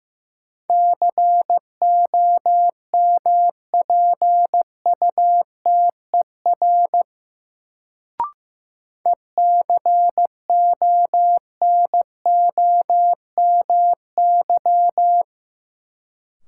Morse code, heard as a keyed tone, about 15 wpm, 700 Hz. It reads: COMPUTER ECONOMY
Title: Sentences from Top 1000 Words - Review Part 3 - New Word Drill - 15wpm